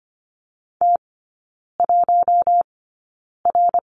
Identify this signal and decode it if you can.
Morse code, heard as a keyed tone, about 25 wpm, spaced as for 9 wpm, 700 Hz. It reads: T1R
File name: Lesson 17e - Letter u - Three Characters 25wpm